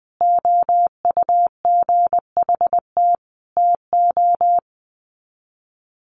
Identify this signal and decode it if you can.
Morse code, heard as a keyed tone, about 20 wpm, 700 Hz. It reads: OUGHT TO
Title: Sets of 5 Words Encore 20wpm